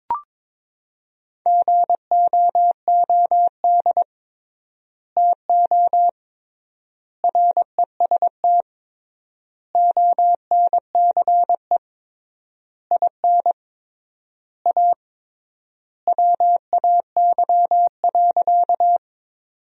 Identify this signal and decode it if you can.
Morse code, heard as a keyed tone, about 22 wpm, 700 Hz. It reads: GOOD TO REST ONCE IN A WAY.
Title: 3x Word-Spacing - Sentences from Top 500 Words - 22wpm